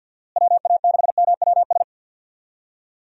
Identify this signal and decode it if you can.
Morse code, heard as a keyed tone, about 52 wpm, 700 Hz. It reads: WA6MWS
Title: USA Call Signs - 6 Characters - Repeated 3x - 50wpm